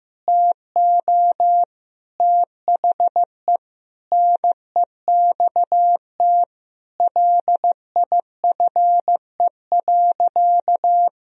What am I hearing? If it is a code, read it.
Morse code, heard as a keyed tone, about 15 wpm, 700 Hz. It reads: TO THE NEXT LIFE.
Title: Sentences from Top 1000 Words - Review Part 1 - 15wpm